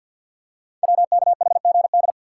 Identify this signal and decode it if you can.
Morse code, heard as a keyed tone, about 52 wpm, 700 Hz. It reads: WKHGD
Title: Code Groups - 5 Letters - 50wpm